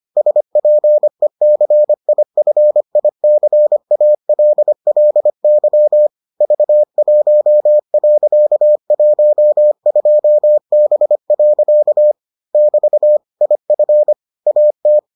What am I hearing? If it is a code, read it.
Morse code, heard as a keyed tone, about 25 wpm, 600 Hz. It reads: SPECIFICALLY V1.12B. <BT> IF AT